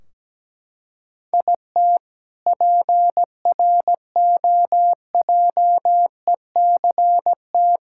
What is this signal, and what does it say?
Morse code, heard as a keyed tone, about 17 wpm, 700 Hz. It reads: IT PROJECT
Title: K1USN SST QSO Practice 17wpm